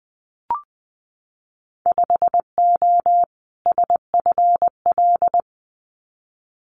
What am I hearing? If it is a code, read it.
Morse code, heard as a keyed tone, about 20 wpm, 700 Hz. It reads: 5O SFL
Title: Winter Field Day Exchanges 20wpm